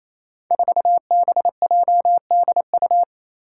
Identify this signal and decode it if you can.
Morse code, heard as a keyed tone, about 28 wpm, 700 Hz. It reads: 4BJDU
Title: Five Challenging Letter-Number Combinations 28wpm